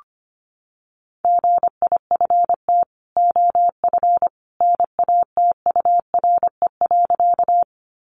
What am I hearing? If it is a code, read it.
Morse code, heard as a keyed tone, about 25 wpm, 700 Hz. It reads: GIFT OF NATURE.
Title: Sentences from Top 900 Words 25wpm